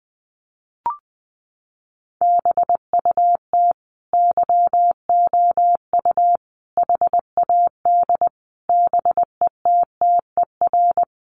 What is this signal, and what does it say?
Morse code, heard as a keyed tone, about 20 wpm, 700 Hz. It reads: BUT YOU HAD BETTER
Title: Sentences from Top 1100 Words 20wpm